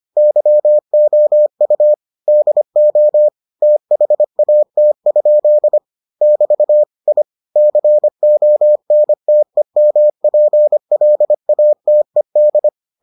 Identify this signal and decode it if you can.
Morse code, heard as a keyed tone, about 25 wpm, 600 Hz. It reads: YOU DO THAT? <BT> I CONTEMPLATED